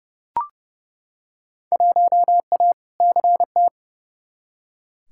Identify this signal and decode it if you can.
Morse code, heard as a keyed tone, about 30 wpm, 700 Hz. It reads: 1A CT
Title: Field Day Exchanges - 30wpm